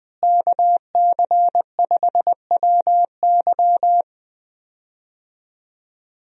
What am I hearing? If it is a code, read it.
Morse code, heard as a keyed tone, about 20 wpm, 700 Hz. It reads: KC5WY
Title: USA Call Signs - 5 Characters - Repeated 3x - 20wpm